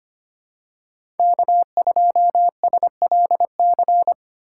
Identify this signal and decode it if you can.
Morse code, heard as a keyed tone, about 25 wpm, 700 Hz. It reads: K2SLC
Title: USA Call Signs - 5 Characters - 2x Character Spacing - 25wpm